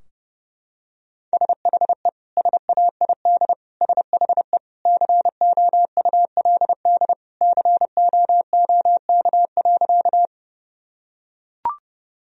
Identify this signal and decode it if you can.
Morse code, heard as a keyed tone, about 30 wpm, 700 Hz. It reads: SHE SAID SHE COULD COOK.
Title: Sentences from Top 1000 Words - Review Part 4 - 30wpm